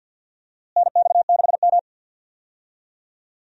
Morse code, heard as a keyed tone, about 49 wpm, 700 Hz. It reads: NX6M
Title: USA Call Signs - 4 Characters - Repeated 2x - 50wpm